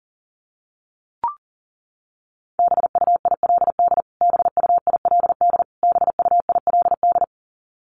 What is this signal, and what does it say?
Morse code, heard as a keyed tone, about 40 wpm, 700 Hz. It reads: BUILD BUILD BUILD
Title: Top 200 Words Repeated 40wpm